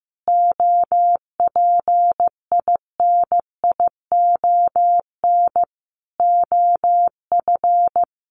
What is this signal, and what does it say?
Morse code, heard as a keyed tone, about 15 wpm, 700 Hz. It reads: OPINION OF
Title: Sentences from Top 1300 Words 15wpm